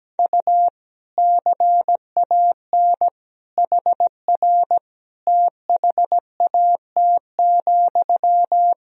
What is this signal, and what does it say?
Morse code, heard as a keyed tone, about 17 wpm, 700 Hz. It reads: U CAN HR THAT,
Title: Sentences from Top 500 Words and Abbreviations 17wpm